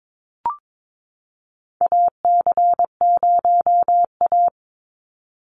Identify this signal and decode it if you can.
Morse code, heard as a keyed tone, about 22 wpm, 700 Hz. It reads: AC0A
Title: NA SOTA Chaser Callsigns 22wpm